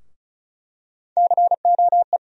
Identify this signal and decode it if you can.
Morse code, heard as a keyed tone, about 36 wpm, 700 Hz. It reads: COE
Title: USA Call Sign Postfixes - 3 Letters 35wpm